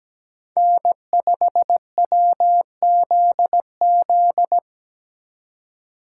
Morse code, heard as a keyed tone, about 20 wpm, 700 Hz. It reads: N5WZZ
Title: Speed Racing - USA 1x3 Call Signs 17wpm